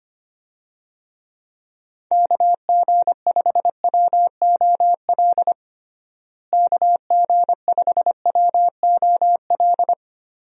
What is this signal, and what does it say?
Morse code, heard as a keyed tone, about 25 wpm, 700 Hz. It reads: KG5WOL KG5WOL
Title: USA Call Signs - 6 Characters - Repeated 3x - 25wpm